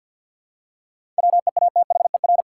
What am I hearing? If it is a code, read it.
Morse code, heard as a keyed tone, about 48 wpm, 700 Hz. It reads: WEATHER
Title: Seven Letter Words - 50wpm